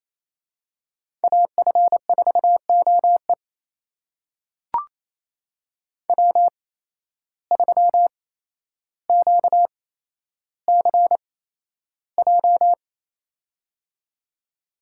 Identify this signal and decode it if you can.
Morse code, heard as a keyed tone, about 28 wpm, spaced as for 12 wpm, 700 Hz. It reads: AF4OE W3QCJ
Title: USA Call Signs - 5 Characters - 8x Character Spacing - 28wpm